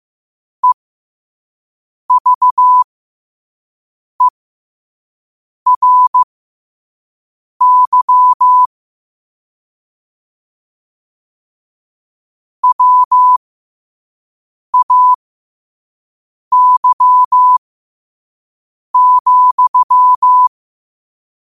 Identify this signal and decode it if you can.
Morse code, heard as a keyed tone, about 15 wpm, spaced as for 5 wpm, 1000 Hz. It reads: EVERY WAY,
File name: qotdfiles / 5wpm